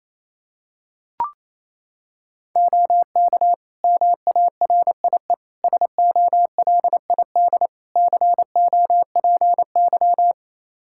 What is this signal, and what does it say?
Morse code, heard as a keyed tone, about 28 wpm, 700 Hz. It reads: OK MARIE SOLID COPY
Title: Practice QSO - Exchange 3 - More Info - 28wpm